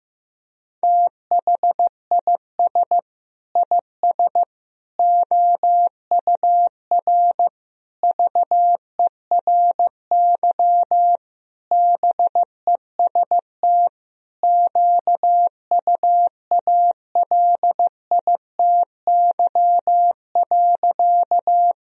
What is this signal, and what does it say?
Morse code, heard as a keyed tone, about 15 wpm, 700 Hz. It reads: THIS IS OUR VERY BEST QUALITY.
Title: Sentences from Top 1000 Words - Review Part 4 - 15wpm